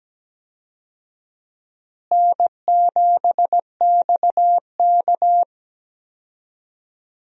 Morse code, heard as a keyed tone, about 17 wpm, 700 Hz. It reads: N7XK
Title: USA Call Signs - 4 Characters - Repeated 3x - 17wpm